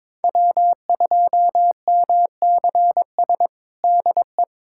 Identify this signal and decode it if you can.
Morse code, heard as a keyed tone, about 22 wpm, 700 Hz. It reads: W2MCS DE
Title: USA Callsign DE Callsign - 5 Characters - 22wpm